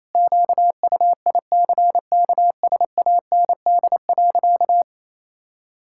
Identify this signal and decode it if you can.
Morse code, heard as a keyed tone, about 27 wpm, 700 Hz. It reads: QUICKSAND.
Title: Pangram Challenge v2 - 28wpm